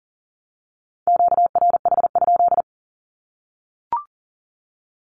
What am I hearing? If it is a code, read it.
Morse code, heard as a keyed tone, about 40 wpm, 700 Hz. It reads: QRH?
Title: Q-Codes 40wpm